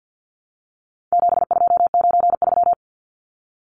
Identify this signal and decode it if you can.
Morse code, heard as a keyed tone, about 50 wpm, 700 Hz. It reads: 7293
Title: Numbers - 4-Digit 50wpm